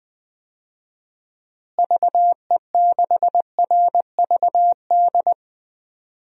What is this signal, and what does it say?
Morse code, heard as a keyed tone, about 20 wpm, 700 Hz. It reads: VE6RVD
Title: DXCC-100 Call Signs - 6 Characters - Repeated 3x - 20wpm